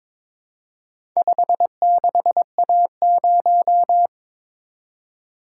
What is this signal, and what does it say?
Morse code, heard as a keyed tone, about 23 wpm, 700 Hz. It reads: 56A0
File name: practice / 22wpm / Four Letter-Number Combinations - 22wpm